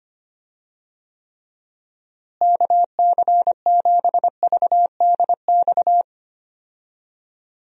Morse code, heard as a keyed tone, about 25 wpm, 700 Hz. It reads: KC7VDX
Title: USA Call Signs - 6 Characters - Repeated 3x - 25wpm